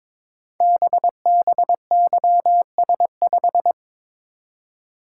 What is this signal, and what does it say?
Morse code, heard as a keyed tone, about 22 wpm, 700 Hz. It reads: BBYS5
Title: Code Groups - 5 Letter:Numbers - 22wpm